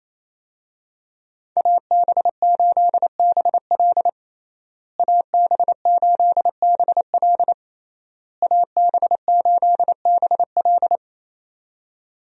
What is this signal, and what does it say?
Morse code, heard as a keyed tone, about 28 wpm, 700 Hz. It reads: AB8BL AB8BL AB8BL
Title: USA Call Signs - 5 Characters - Repeated 3x - 28wpm